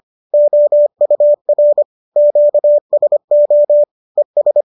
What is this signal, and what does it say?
Morse code, heard as a keyed tone, about 25 wpm, 590 Hz. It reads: OUR QSO ES